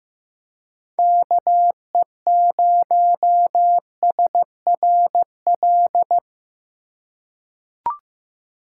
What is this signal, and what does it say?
Morse code, heard as a keyed tone, about 15 wpm, 700 Hz. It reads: KE0SRL
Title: K1USN SST QSO Practice - 15wpm